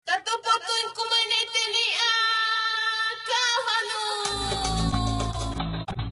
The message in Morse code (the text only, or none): none